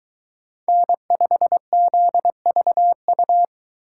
Morse code, heard as a keyed tone, about 23 wpm, 700 Hz. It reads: N5ZVU